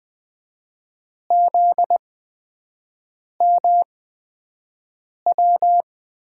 Morse code, heard as a keyed tone, about 20 wpm, spaced as for 5 wpm, 700 Hz. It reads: ZMW
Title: USA Call Signs - 6 Characters - 8x Character Spacing - 20wpm